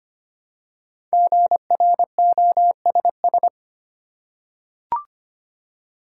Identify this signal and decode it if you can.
Morse code, heard as a keyed tone, about 25 wpm, 700 Hz. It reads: GROSS
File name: 5 Letter Words - First 3 Letters Given - 25wpm